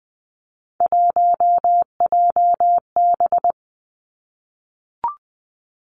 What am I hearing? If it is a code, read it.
Morse code, heard as a keyed tone, about 20 wpm, 700 Hz. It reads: 1JB
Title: Three Challenging Letter-Number Combinations 20wpm